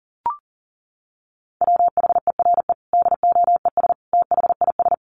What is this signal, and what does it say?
Morse code, heard as a keyed tone, about 40 wpm, 700 Hz. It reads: WHERE DOES THIS
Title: Sentences from Top 1400 Words 40wpm